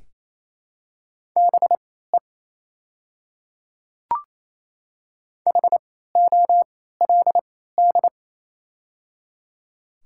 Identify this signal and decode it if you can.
Morse code, heard as a keyed tone, about 28 wpm, spaced as for 16 wpm, 700 Hz. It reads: BE HOLD
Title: 3x Character Spacing - Top 500 Words - 28wpm